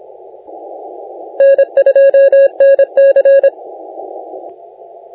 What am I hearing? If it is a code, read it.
Morse code, heard as a keyed tone, about 27 wpm, 550 Hz. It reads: N2NC
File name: la4c / sac-cw-2018 / CONTEST AUDIO 1600Z